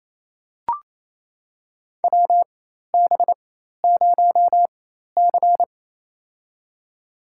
Morse code, heard as a keyed tone, about 28 wpm, spaced as for 13 wpm, 700 Hz. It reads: WB0C
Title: USA Call Signs - 4x Character Spacing - 28wpm